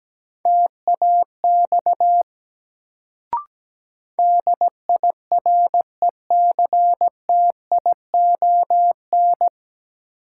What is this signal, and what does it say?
Morse code, heard as a keyed tone, about 17 wpm, 700 Hz. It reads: TAX DIRECTION
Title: Sentences from Top 1000 Words - Review Part 5 - New Word Drill - 17wpm